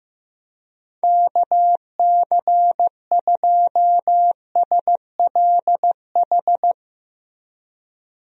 Morse code, heard as a keyed tone, about 16 wpm, 700 Hz. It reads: KC2SLH
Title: USA Call Signs - 6 Characters - 15wpm